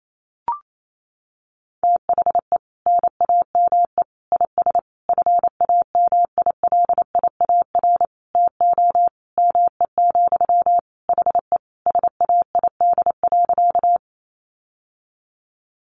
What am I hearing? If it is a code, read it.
Morse code, heard as a keyed tone, about 28 wpm, 700 Hz. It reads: THE NAME IS FAMILIAR TO ME, HE SAID.
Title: Sentences from Top 1000 Words 28wpm